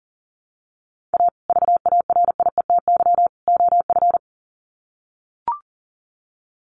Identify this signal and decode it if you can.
Morse code, heard as a keyed tone, about 40 wpm, 700 Hz. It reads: A VARIETY OF
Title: Sets of 3 Words 40wpm